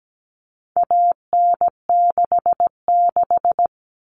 Morse code, heard as a keyed tone, about 17 wpm, 700 Hz. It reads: AN66
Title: All Cut Numbers - 4-Digit 17wpm